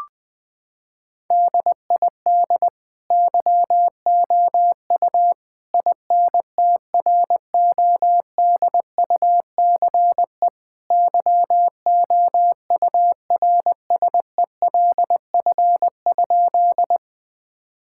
Morse code, heard as a keyed tone, about 20 wpm, 700 Hz. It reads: DID YOU INTRODUCE YOURSELF?
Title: Sentences from Top 1600 Words 20wpm